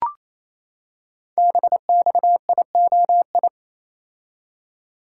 Morse code, heard as a keyed tone, about 27 wpm, 700 Hz. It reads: BXIOI
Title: Code Groups - 5 Letters - 28wpm